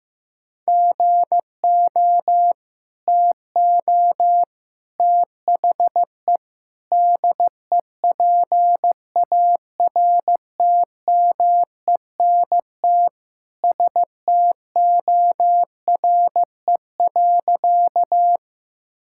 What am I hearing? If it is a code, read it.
Morse code, heard as a keyed tone, about 15 wpm, 700 Hz. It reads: GO TO THE DEPARTMENT STORE.
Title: Sentences from Top 1000 Words - Review Part 5 - 15wpm